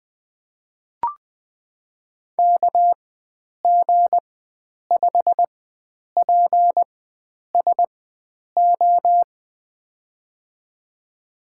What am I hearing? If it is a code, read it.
Morse code, heard as a keyed tone, about 21 wpm, spaced as for 9 wpm, 700 Hz. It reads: KG5PSO
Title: USA Call Signs - 6 Characters - 4x Character Spacing - 20wpm